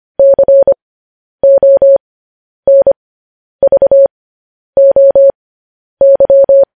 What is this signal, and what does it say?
Morse code, heard as a keyed tone, about 25 wpm, spaced as for 10 wpm, 550 Hz. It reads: CONVOY
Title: News Headlines at 10